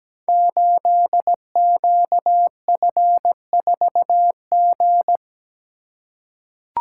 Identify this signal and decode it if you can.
Morse code, heard as a keyed tone, about 17 wpm, 700 Hz. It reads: 8QF4G
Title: Code Groups - 5 Letter:Numbers - 17wpm